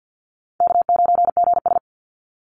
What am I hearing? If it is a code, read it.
Morse code, heard as a keyed tone, about 50 wpm, 700 Hz. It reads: K9GS
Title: Call Signs - 50wpm